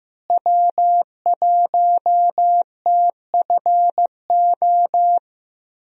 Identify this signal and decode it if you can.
Morse code, heard as a keyed tone, about 15 wpm, 700 Hz. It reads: W1TFO
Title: USA Call Signs - 5 Characters - Repeated 2x - 15wpm